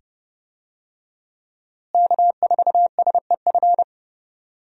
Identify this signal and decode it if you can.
Morse code, heard as a keyed tone, about 30 wpm, 700 Hz. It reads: K4SEF
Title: USA Call Signs - 5 Characters - Repeated 2x - 30wpm